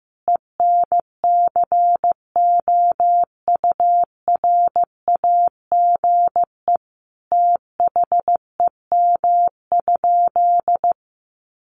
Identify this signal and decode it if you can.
Morse code, heard as a keyed tone, about 15 wpm, 700 Hz. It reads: ENCOURAGE THEM?
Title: Sentences from Top 1400 Words 15wpm